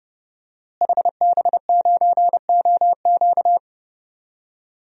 Morse code, heard as a keyed tone, about 30 wpm, 700 Hz. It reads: HB9OQ
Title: DXCC-100 Call Signs - 5 Characters - Repeated 3x - 30wpm